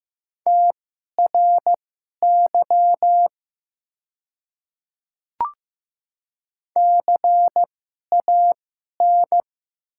Morse code, heard as a keyed tone, about 15 wpm, spaced as for 11 wpm, 700 Hz. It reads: TRY CAN
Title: 2x Character Spacing - Top 300 Words - 15wpm